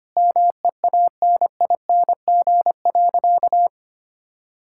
Morse code, heard as a keyed tone, about 25 wpm, 700 Hz. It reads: MEANING.
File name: practice / 25wpm / Sentences with 2-Word Phrases - 25wpm